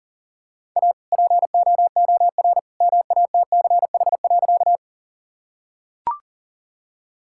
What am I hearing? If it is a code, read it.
Morse code, heard as a keyed tone, about 40 wpm, 700 Hz. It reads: A POOR MATCH.
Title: Sentences from Top 1600 Words 40wpm